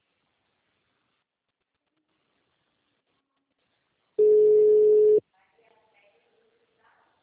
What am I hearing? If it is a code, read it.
Ringback tone (Europe, 425 Hz), with speech.